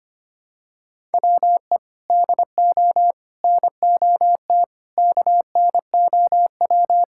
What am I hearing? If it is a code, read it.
Morse code, heard as a keyed tone, about 25 wpm, 700 Hz. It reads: WE DO NOT KNOW
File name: practice / 25wpm / Sentences with 4-Word Phrases - 25wpm